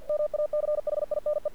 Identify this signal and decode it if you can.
Morse code, heard as a keyed tone, about 49 wpm, 600 Hz. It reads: MACHINE
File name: archived / CRSnet